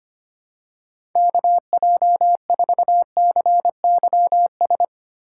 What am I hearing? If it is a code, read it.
Morse code, heard as a keyed tone, about 25 wpm, 700 Hz. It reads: KJ4CYS
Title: USA Call Signs - 6 Characters - 2x Character Spacing - 25wpm